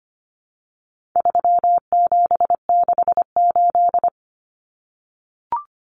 Morse code, heard as a keyed tone, about 25 wpm, 700 Hz. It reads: 3768
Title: Numbers - 4-Digit 25wpm